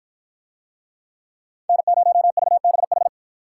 Morse code, heard as a keyed tone, about 50 wpm, 700 Hz. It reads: N0VBH